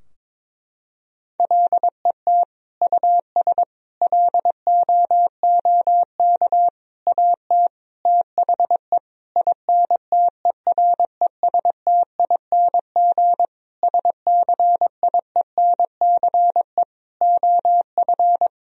Morse code, heard as a keyed tone, about 22 wpm, 700 Hz. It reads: LET US LOOK AT THE INTERESTING SCIENCE OF
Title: Sentences from Top 1000 Words - Review Part 2 - 22wpm